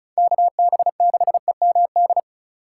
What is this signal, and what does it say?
Morse code, heard as a keyed tone, about 44 wpm, 700 Hz. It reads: KB6EMD
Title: Speed Racing - USA 2x3 Call Signs 35wpm